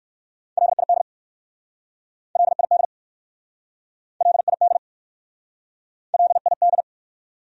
Morse code, heard as a keyed tone, about 60 wpm, 700 Hz. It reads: LID LID LID LID